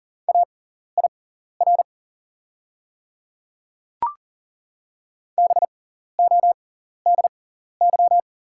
Morse code, heard as a keyed tone, about 40 wpm, spaced as for 14 wpm, 690 Hz. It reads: AIR BODY
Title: 6x Character Spacing - Top 500 Words - 40wpm